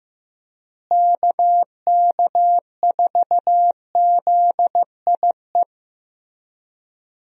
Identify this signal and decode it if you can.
Morse code, heard as a keyed tone, about 15 wpm, 700 Hz. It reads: KK4ZIE
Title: USA Call Signs - 6 Characters - Repeated 3x - 15wpm